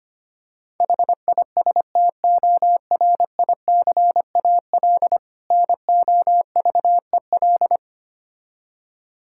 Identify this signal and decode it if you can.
Morse code, heard as a keyed tone, about 25 wpm, 700 Hz. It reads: HISTORICAL NOVEL
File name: practice / 25wpm / Difficult Words with Hints - 25wpm